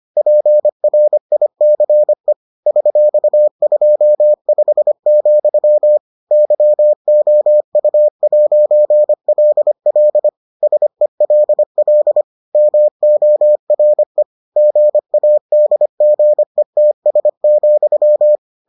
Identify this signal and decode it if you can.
Morse code, heard as a keyed tone, about 25 wpm, 600 Hz. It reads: PRICE $25, YOU'LL SELL MORE GADGETS,